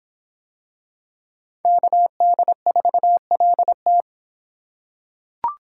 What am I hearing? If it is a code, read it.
Morse code, heard as a keyed tone, about 25 wpm, 700 Hz. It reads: KD4LT